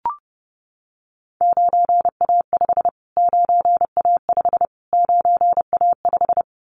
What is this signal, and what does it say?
Morse code, heard as a keyed tone, about 30 wpm, 700 Hz. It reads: 9A5 9A5 9A5
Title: Callsign Prefixes for Top 25 DXCC - Repeated 3x - 30wpm